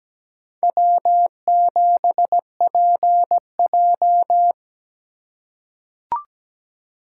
Morse code, heard as a keyed tone, about 17 wpm, 700 Hz. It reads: W7PJ
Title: USA Call Signs - 4 Characters - 3x Character Spacing - 17wpm